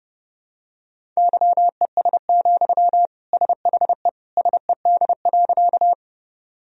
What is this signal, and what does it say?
Morse code, heard as a keyed tone, about 30 wpm, 700 Hz. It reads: YES, SHE SED.
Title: Sentences from Top 500 Words and Abbreviations 30wpm